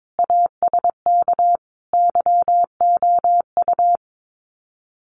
Morse code, heard as a keyed tone, about 22 wpm, 700 Hz. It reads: ASK YOU
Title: Sets of 5 Words Encore2 22wpm